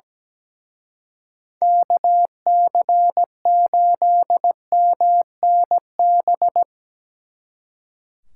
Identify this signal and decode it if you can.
Morse code, heard as a keyed tone, about 19 wpm, 700 Hz. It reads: KC8MNB
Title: Speed Racing - USA 2x3 Call Signs 17wpm